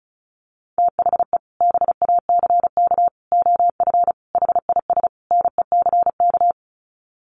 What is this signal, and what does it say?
Morse code, heard as a keyed tone, about 34 wpm, 700 Hz. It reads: THE BACK OF HIS NECK
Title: Sets of 5 Words Encore 35wpm